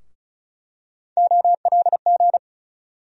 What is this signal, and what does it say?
Morse code, heard as a keyed tone, about 36 wpm, 700 Hz. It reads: OLG
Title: USA Call Sign Postfixes - 3 Letters 35wpm